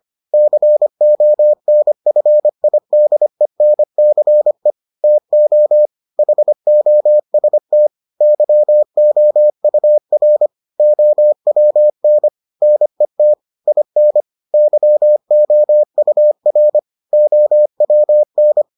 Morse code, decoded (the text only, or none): CONFIDENCE TO HOST YOUR OWN NET IN YOUR OWN